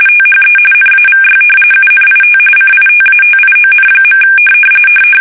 RTTY, about 45.45 baud. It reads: THE NAVY UNIT CITATION.
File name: fsk850 / battleshipnj